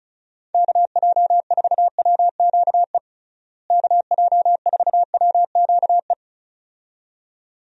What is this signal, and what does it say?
Morse code, heard as a keyed tone, about 36 wpm, 700 Hz. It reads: KJ4WQE KJ4WQE
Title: USA Call Signs - 6 Characters - Repeated 2x - 35wpm